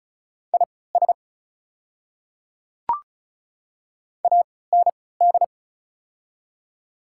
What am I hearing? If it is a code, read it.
Morse code, heard as a keyed tone, about 36 wpm, spaced as for 20 wpm, 700 Hz. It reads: IS AND
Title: 3x Character Spacing - Top 100 Words - 35wpm